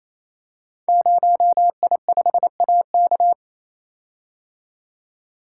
Morse code, heard as a keyed tone, about 27 wpm, 700 Hz. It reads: 0I5AK
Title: Code Groups - 5 Letter:Numbers - 28wpm